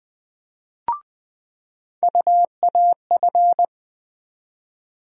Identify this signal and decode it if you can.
Morse code, heard as a keyed tone, about 20 wpm, 700 Hz. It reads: UAF
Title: USA Call Sign Postfixes - 3 Letters 20wpm